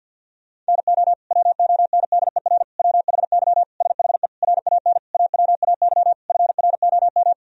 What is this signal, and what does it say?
Morse code, heard as a keyed tone, about 50 wpm, 700 Hz. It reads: NO WONDER WHY SHE RAN AWAY FROM
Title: Sentences from Top 1000 Words - Review Part 3 - 50wpm